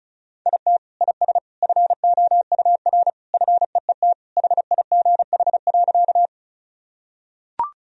Morse code, heard as a keyed tone, about 36 wpm, 690 Hz. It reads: IT IS FOUR FEET HIGH.
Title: Sentences from Top 1000 Words - Review Part 1 - 35wpm